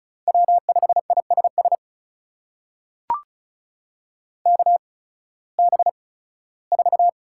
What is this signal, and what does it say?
Morse code, heard as a keyed tone, about 36 wpm, spaced as for 19 wpm, 700 Hz. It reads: W5ISS KB4
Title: USA Call Signs - 8x Character Spacing - 35wpm